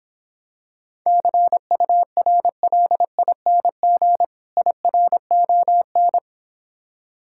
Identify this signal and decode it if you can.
Morse code, heard as a keyed tone, about 26 wpm, 700 Hz. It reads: CURLING IRON